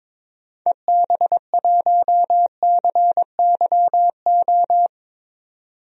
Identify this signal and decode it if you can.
Morse code, heard as a keyed tone, about 22 wpm, 700 Hz. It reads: EB1CYO
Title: DXCC-100 Call Signs - 6 Characters - 22wpm